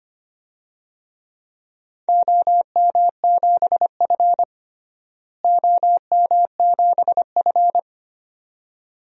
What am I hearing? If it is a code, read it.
Morse code, heard as a keyed tone, about 25 wpm, 700 Hz. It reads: OM7F OM7F
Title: DXCC-100 Call Signs - 4 Characters - Repeated 2x - 25wpm